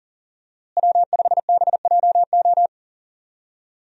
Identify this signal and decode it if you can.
Morse code, heard as a keyed tone, about 40 wpm, 700 Hz. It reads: W5BJO
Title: USA Call Signs - 5 Characters - Repeated 3x - 40wpm